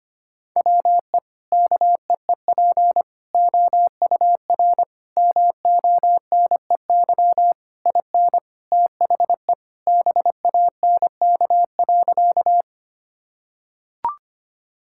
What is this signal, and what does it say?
Morse code, heard as a keyed tone, about 25 wpm, 700 Hz. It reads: WE KEEP OUR MONEY IN THE BANK.
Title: Sentences from Top 1000 Words - Review Part 6 - 25wpm